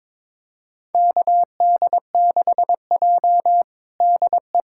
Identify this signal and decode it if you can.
Morse code, heard as a keyed tone, about 22 wpm, 700 Hz. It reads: KD6J DE
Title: USA Callsign DE Callsign - 4 Characters - 22wpm